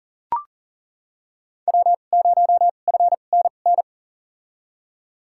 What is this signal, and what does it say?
Morse code, heard as a keyed tone, about 40 wpm, spaced as for 29 wpm, 700 Hz. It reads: W0FNN
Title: USA Call Signs - 5 Characters - 2x Character Spacing - 40wpm